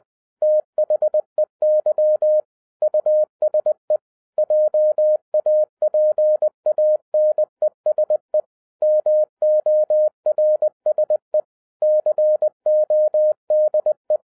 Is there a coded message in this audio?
Yes: Morse code from 0.4 to 14.2 s, heard as a keyed tone, about 20 wpm, 610 Hz. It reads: THEY USE JAPANESE MORSE CODE